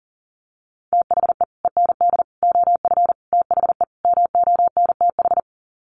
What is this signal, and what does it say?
Morse code, heard as a keyed tone, about 40 wpm, 700 Hz. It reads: THE END OF THE MONTH
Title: Sets of 5 Words Encore 40wpm